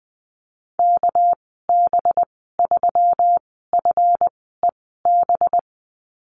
Morse code, heard as a keyed tone, about 20 wpm, spaced as for 15 wpm, 700 Hz. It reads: KB3FEB
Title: 231 Lesson 37g - Letter x - Call Signs 20wpm